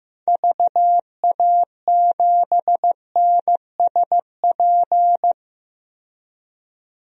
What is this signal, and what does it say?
Morse code, heard as a keyed tone, about 15 wpm, 700 Hz. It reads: VA7NSP
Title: DXCC-100 Call Signs - 6 Characters - 15wpm